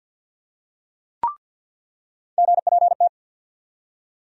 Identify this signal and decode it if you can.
Morse code, heard as a keyed tone, about 53 wpm, 700 Hz. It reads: GPT
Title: USA Call Sign Postfixes - 3 Letters 50wpm